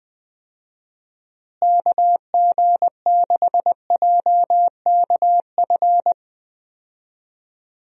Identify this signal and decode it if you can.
Morse code, heard as a keyed tone, about 20 wpm, 700 Hz. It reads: KG6JKF
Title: USA Call Signs - 6 Characters - Repeated 2x - 20wpm